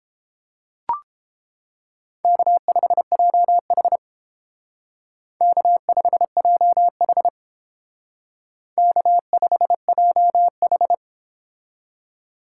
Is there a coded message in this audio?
Yes: Morse code from 2.2 to 10.9 s, heard as a keyed tone, about 29 wpm, 700 Hz. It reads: K5JH K5JH K5JH